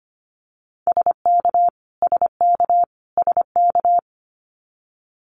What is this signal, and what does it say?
Morse code, heard as a keyed tone, about 25 wpm, 700 Hz. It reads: SK SK SK
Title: Canadian Provinces and Territories - Repeated 3x - 25wpm